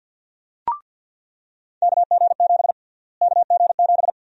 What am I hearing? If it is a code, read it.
Morse code, heard as a keyed tone, about 49 wpm, 700 Hz. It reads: KG7 KG7
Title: USA Call Sign Prefixes - Repeated 2x - 50wpm